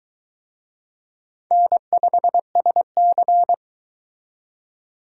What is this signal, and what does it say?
Morse code, heard as a keyed tone, about 23 wpm, 700 Hz. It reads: N5SC